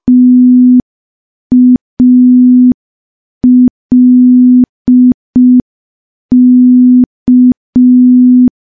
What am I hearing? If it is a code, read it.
Morse code, heard as a keyed tone, about 5 wpm, 250 Hz. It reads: TALK
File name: morse / morse-training-05WPM